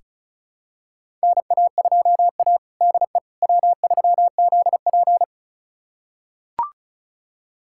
Morse code, heard as a keyed tone, about 36 wpm, 700 Hz. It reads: NA2A DE W3ZP
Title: USA Callsign DE Callsign - 4 Characters - 35wpm